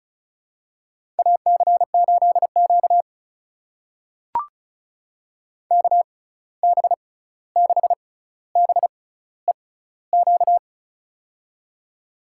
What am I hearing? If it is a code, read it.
Morse code, heard as a keyed tone, about 36 wpm, spaced as for 16 wpm, 700 Hz. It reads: AC8Q KB6BEQ
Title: USA Call Signs - 6x Character Spacing - 35wpm